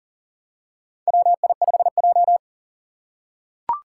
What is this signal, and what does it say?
Morse code, heard as a keyed tone, about 43 wpm, 700 Hz. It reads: WI5J
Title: Four Challenging Letter-Number Combinations 40wpm